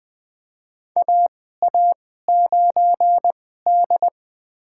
Morse code, heard as a keyed tone, about 20 wpm, spaced as for 14 wpm, 700 Hz. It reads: AA9D
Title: USA Call Signs - 4 Characters - 2x Character Spacing - 20wpm